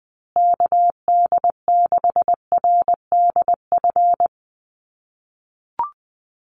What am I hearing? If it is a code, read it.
Morse code, heard as a keyed tone, about 20 wpm, 700 Hz. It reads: KD6RDF
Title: USA Call Signs Repeated 3x 20wpm